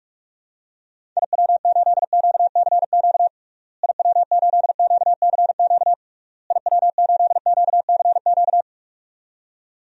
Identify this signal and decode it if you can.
Morse code, heard as a keyed tone, about 41 wpm, 700 Hz. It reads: IW8QCQ IW8QCQ IW8QCQ
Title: DXCC-100 Call Signs - 6 Characters - Repeated 3x - 45wpm